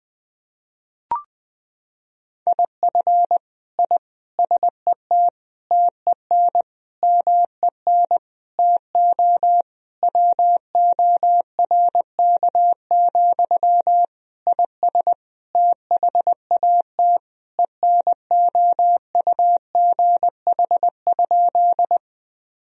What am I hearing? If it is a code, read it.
Morse code, heard as a keyed tone, about 21 wpm, 700 Hz. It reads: IF I SET TEN MEN TO WORK, IS THAT ENOUGH?
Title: Sentences from Top 1000 Words - Review Part 2 - 20wpm